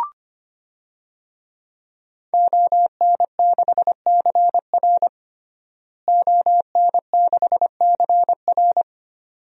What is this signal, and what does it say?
Morse code, heard as a keyed tone, about 25 wpm, 700 Hz. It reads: ON6CR ON6CR
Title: DXCC-100 Call Signs - 5 Characters - Repeated 2x - 25wpm